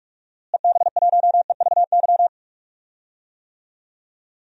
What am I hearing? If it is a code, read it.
Morse code, heard as a keyed tone, about 43 wpm, 700 Hz. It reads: EB1EVY
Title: DXCC-100 Call Signs - 6 Characters - Repeated 3x - 45wpm